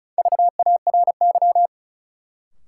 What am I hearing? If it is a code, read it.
Morse code, heard as a keyed tone, about 36 wpm, 700 Hz. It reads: VARY
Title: Sentences from Top 1200 Words - New Word Drill - 35wpm